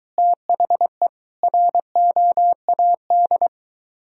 Morse code, heard as a keyed tone, about 23 wpm, 700 Hz. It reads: THE ROAD